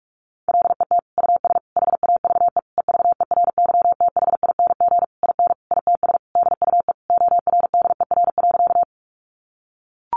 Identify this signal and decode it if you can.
Morse code, heard as a keyed tone, about 44 wpm, 700 Hz. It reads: LET US HAVE EVERYTHING IN ITS DUE ORDER.
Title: Sentences from Top 900 Words 45wpm